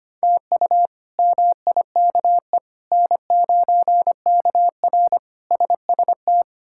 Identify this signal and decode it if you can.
Morse code, heard as a keyed tone, about 25 wpm, 700 Hz. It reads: TU MIKE N9KR SST
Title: K1USN SST QSO Practice - 25wpm